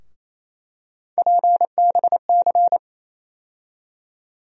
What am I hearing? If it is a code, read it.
Morse code, heard as a keyed tone, about 28 wpm, 700 Hz. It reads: PBC
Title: USA Call Sign Postfixes - 3 Letters 28wpm